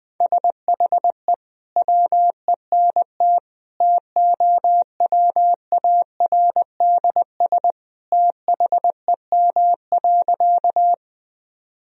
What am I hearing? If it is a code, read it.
Morse code, heard as a keyed tone, about 20 wpm, 700 Hz. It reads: SHE WENT TOWARDS THEM.
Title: Sentences from Top 1000 Words - Review Part 2 - 20wpm